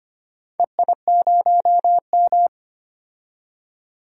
Morse code, heard as a keyed tone, about 25 wpm, 700 Hz. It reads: EI0M
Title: DXCC-100 Call Signs - 4 Characters - 25wpm